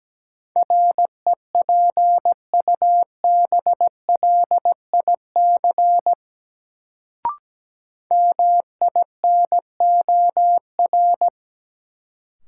Morse code, heard as a keyed tone, about 17 wpm, 700 Hz. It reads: REPUBLIC MINOR
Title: Sentences from Top 1500 Words - New Word Drill - 17wpm